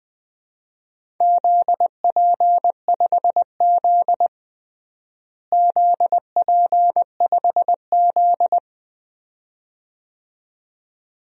Morse code, heard as a keyed tone, about 20 wpm, 700 Hz. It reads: ZP5Z ZP5Z
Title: DXCC-100 Call Signs - 4 Characters - Repeated 2x - 20wpm